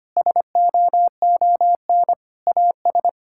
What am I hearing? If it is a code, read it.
Morse code, heard as a keyed tone, about 25 wpm, 700 Hz. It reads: SOON AS
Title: Sentences from Top 1000 Words - Review Part 5 - 25wpm